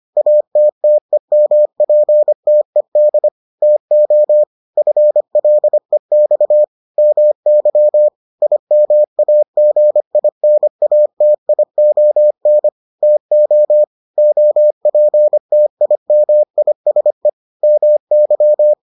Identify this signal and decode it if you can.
Morse code, heard as a keyed tone, about 25 wpm, 600 Hz. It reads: ATTEMPTED TO FLEX MY IMAGINATION TO OPTIMISE MY